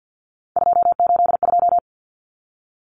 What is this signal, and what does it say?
Morse code, heard as a keyed tone, about 50 wpm, 700 Hz. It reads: 282
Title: Numbers - 3-Digit 50wpm